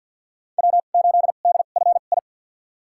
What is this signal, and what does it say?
Morse code, heard as a keyed tone, about 52 wpm, spaced as for 36 wpm, 700 Hz. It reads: W8DFI
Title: USA Call Signs - 5 Characters - 2x Character Spacing - 50wpm